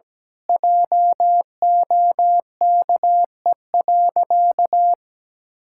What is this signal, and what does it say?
Morse code, heard as a keyed tone, about 17 wpm, 700 Hz. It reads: JOKE.
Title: Pangram Challenge v2 - 17wpm